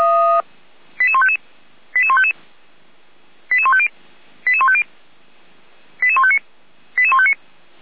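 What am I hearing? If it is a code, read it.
Five-tone selective call (ZVEI), heard as quick runs of five tones, tones of about 70 ms. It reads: 88155 88155 88150 88150 88159 88159
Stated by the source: ZVEI five-tone selective call received by SDR.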